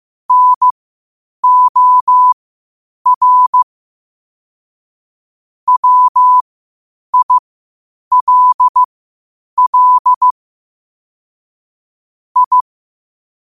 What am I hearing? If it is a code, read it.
Morse code, heard as a keyed tone, about 15 wpm, spaced as for 8 wpm, 1000 Hz. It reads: NOR WILL I
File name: qotdfiles / 8wpm